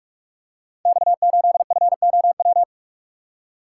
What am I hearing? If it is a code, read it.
Morse code, heard as a keyed tone, about 44 wpm, 700 Hz. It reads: X8FOW
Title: Code Groups - 5 Letter:Numbers - 45wpm